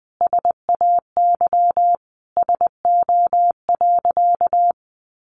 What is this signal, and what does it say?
Morse code, heard as a keyed tone, about 19 wpm, 700 Hz. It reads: SAY SO.
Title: Sentences from Top 1400 Words 20wpm